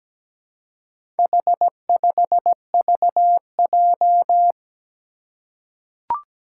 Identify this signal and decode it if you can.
Morse code, heard as a keyed tone, about 17 wpm, 700 Hz. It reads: H5VJ
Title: Four Challenging Letter-Number Combinations 17wpm